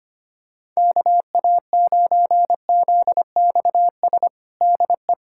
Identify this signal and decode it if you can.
Morse code, heard as a keyed tone, about 25 wpm, 700 Hz. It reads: KA9ZXS DE
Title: USA Callsign DE Callsign - 6 Characters - 25wpm